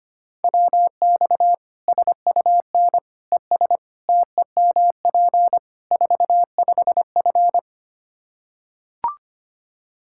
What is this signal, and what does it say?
Morse code, heard as a keyed tone, about 25 wpm, 700 Hz. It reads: WX SUN ES TEMP 45F
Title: Practice QSO - Exchange 2 - Info - 25wpm